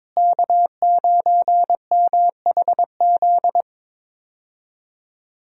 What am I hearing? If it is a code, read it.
Morse code, heard as a keyed tone, about 22 wpm, 700 Hz. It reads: K9MHZ
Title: USA Call Signs - 5 Characters - Repeated 2x - 22wpm